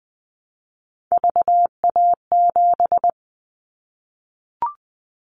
Morse code, heard as a keyed tone, about 19 wpm, 700 Hz. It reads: VA7
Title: Callsign Prefixes for Top 50 DXCC - 20wpm